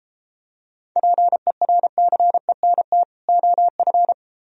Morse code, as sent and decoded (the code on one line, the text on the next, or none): .--. .-. -.-. -. - / --- ..-.
PERCENT OF